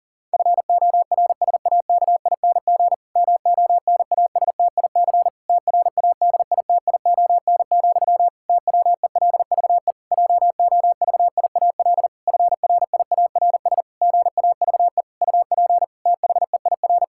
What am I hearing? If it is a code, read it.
Morse code, heard as a keyed tone, about 40 wpm, 700 Hz. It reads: FORSAKING MONASTIC TRADITION, TWELVE JOVIAL FRIARS GAVE UP THEIR
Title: Pangram Challenge v2 - 40wpm